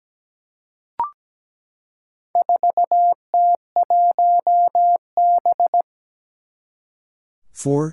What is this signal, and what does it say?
Morse code, heard as a keyed tone, about 17 wpm, 700 Hz. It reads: 4T1B
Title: Four Challenging Letter-Number Combinations 17wpm